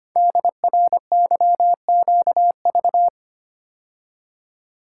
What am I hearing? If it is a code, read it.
Morse code, heard as a keyed tone, about 25 wpm, 700 Hz. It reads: DRYQV
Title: Code Groups - 5 Letters - 25wpm